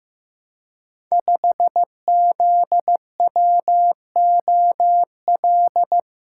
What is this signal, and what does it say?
Morse code, heard as a keyed tone, about 15 wpm, 700 Hz. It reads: 5ZWOL
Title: Code Groups - 5 Letter:Numbers - 15wpm